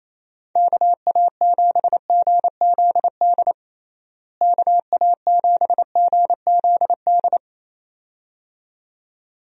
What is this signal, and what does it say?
Morse code, heard as a keyed tone, about 27 wpm, 700 Hz. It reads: KA7GZD KA7GZD
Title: USA Call Signs - 6 Characters - Repeated 2x - 28wpm